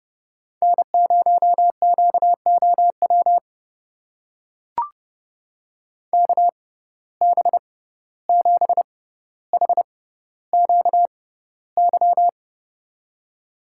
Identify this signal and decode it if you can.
Morse code, heard as a keyed tone, about 30 wpm, spaced as for 15 wpm, 700 Hz. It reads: N0QOW KB7HQY
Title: USA Call Signs - 6x Character Spacing - 30wpm